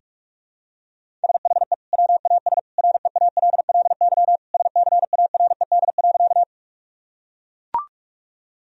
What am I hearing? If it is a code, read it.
Morse code, heard as a keyed tone, about 41 wpm, 700 Hz. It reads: SHE WAS REALLY SCARED.
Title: Sentences from Top 1000 Words - Review Part 4 - 45wpm